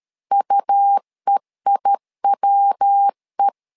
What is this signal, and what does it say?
Morse code, heard as a keyed tone, about 13 wpm, 800 Hz. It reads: UEIWE
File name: LEZIONI / 3 MWRO / 3 MW 12wpm 1633